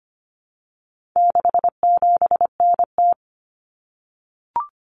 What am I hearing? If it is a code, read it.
Morse code, heard as a keyed tone, about 25 wpm, 700 Hz. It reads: 67NT
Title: Common Cut Numbers - 4-Digit 25wpm